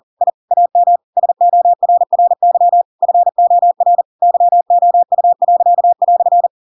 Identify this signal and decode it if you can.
Morse code, heard as a keyed tone, about 40 wpm, 700 Hz. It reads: I AM SORRY FOR YOU."
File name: Pinocchio / Pinocchio ch4 40 40 ik1hge